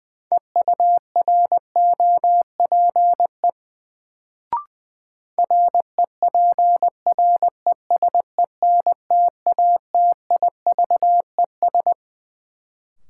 Morse code, heard as a keyed tone, about 20 wpm, 700 Hz. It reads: EUROPE REPRESENTATIVES
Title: Sets of 5 Words Encore - New Word Drill - 20wpm